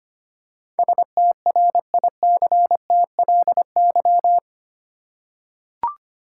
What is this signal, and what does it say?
Morse code, heard as a keyed tone, about 25 wpm, 700 Hz. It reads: STRICTLY
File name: 8 Letter Words - First 3 Letters Given - 25wpm